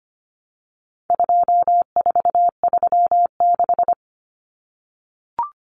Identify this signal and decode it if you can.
Morse code, heard as a keyed tone, about 25 wpm, 700 Hz. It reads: 2436
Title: Numbers - 4-Digit 25wpm